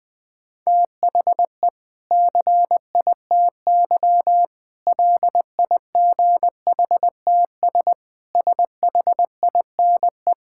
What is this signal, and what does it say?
Morse code, heard as a keyed tone, about 20 wpm, 700 Hz. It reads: THE CITY LIGHTS SHINE
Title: Difficult Sentences with Hints - 20wpm